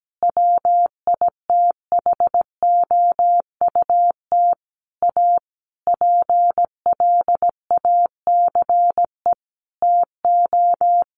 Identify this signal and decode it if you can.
Morse code, heard as a keyed tone, about 17 wpm, 700 Hz. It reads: WITHOUT A PLACE TO
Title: Sentences from Top 600 Words 17wpm